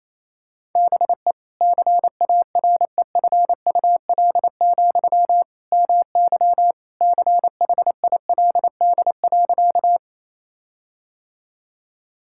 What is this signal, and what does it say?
Morse code, heard as a keyed tone, about 28 wpm, 700 Hz. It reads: BE CAREFUL, MY CHILD.
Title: Sentences from Top 1600 Words 28wpm